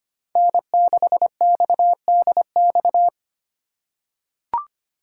Morse code, heard as a keyed tone, about 25 wpm, 700 Hz. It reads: N6XDX
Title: USA Call Signs - 5 Characters - 6x Character Spacing - 25wpm